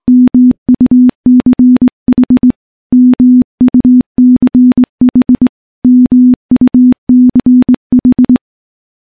Morse code, heard as a keyed tone, about 21 wpm, 250 Hz. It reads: MUCH MUCH MUCH
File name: morse / morse-training-20WPM